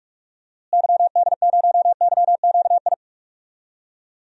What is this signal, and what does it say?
Morse code, heard as a keyed tone, about 46 wpm, 700 Hz. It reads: YD0YQI
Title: DXCC-100 Call Signs - 6 Characters - Repeated 2x - 45wpm